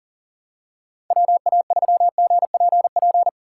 Morse code, heard as a keyed tone, about 37 wpm, 700 Hz. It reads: WA3GPP
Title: K1USN SST QSO Practice - 40wpm